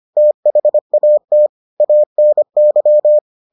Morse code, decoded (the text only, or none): THAT ANY